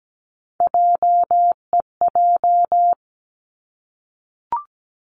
Morse code, heard as a keyed tone, about 17 wpm, 700 Hz. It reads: JEJ
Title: Three Challenging Letter-Number Combinations 17wpm